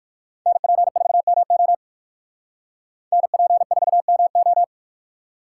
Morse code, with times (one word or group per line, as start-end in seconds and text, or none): 0.5-1.7 NP4MO
3.1-4.6 NP4MO